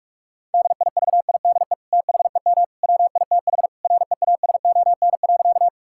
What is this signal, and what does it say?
Morse code, heard as a keyed tone, about 46 wpm, 700 Hz. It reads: DIVIDE THEM WITH REASON.
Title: Sentences from Top 1600 Words 45wpm